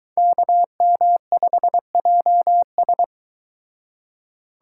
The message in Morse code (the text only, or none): KM5JS